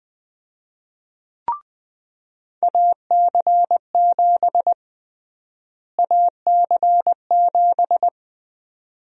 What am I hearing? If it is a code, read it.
Morse code, heard as a keyed tone, about 20 wpm, 700 Hz. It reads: AC7 AC7
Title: USA Call Sign Prefixes - Repeated 2x - 20wpm